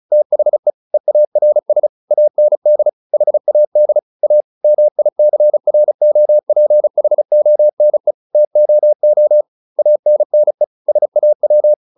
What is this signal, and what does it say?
Morse code, heard as a keyed tone, about 36 wpm, 600 Hz. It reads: THE EARS AND HAD A MICROPHONE TOO ANNE SAW
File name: SS303 35